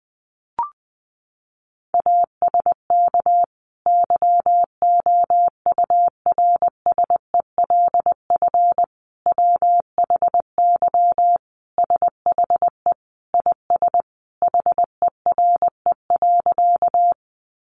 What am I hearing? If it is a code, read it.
Morse code, heard as a keyed tone, about 20 wpm, 700 Hz. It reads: ASK YOURSELF WHY SHE IS HERE.
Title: Sentences from Top 600 Words 20wpm